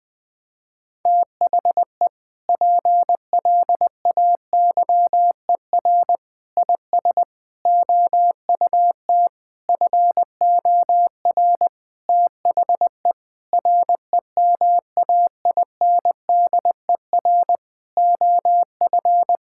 Morse code, heard as a keyed tone, about 20 wpm, 700 Hz. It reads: THE PLAYER IS OUT FOR THE REMAINDER OF
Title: Sentences with 6-Word Phrases - 20wpm